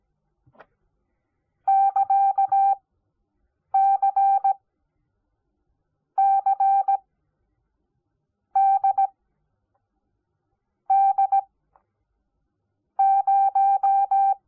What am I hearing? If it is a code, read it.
Morse code, heard as a keyed tone, about 17 wpm, spaced as for 5 wpm, 780 Hz. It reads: <KA> CCDD0
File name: Onlinelehrgang / morsen / DJ4UF Morsen05